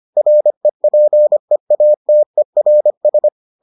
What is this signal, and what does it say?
Morse code, heard as a keyed tone, about 25 wpm, 600 Hz. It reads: REPEATERS